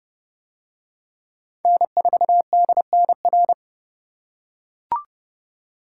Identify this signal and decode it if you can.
Morse code, heard as a keyed tone, about 29 wpm, 700 Hz. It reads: N4DNR